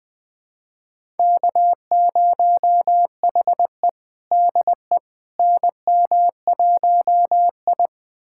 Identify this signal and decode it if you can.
Morse code, heard as a keyed tone, about 20 wpm, 700 Hz. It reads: K0HE DE NM1I
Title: USA Callsign DE Callsign - 4 Characters - 20wpm